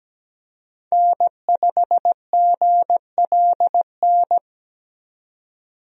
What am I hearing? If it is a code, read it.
Morse code, heard as a keyed tone, about 17 wpm, 700 Hz. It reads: N5GLN
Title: USA Call Signs - 5 Characters - 17wpm